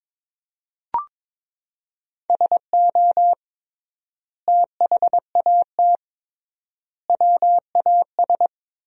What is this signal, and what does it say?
Morse code, heard as a keyed tone, about 22 wpm, 700 Hz. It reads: SO THAT WAS
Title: 3x Word-Spacing - Sentences from Top 100 Words - 22wpm